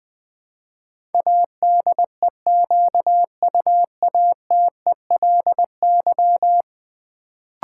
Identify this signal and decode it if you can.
Morse code, heard as a keyed tone, about 21 wpm, 700 Hz. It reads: ADEQUATELY
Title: Build A Word - 4 Letters At A Time - 10-Letter Words - 20wpm